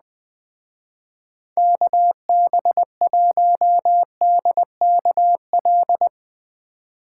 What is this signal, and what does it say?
Morse code, heard as a keyed tone, about 20 wpm, 700 Hz. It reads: KB1DKL
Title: Speed Racing - USA 2x3 Call Signs 17wpm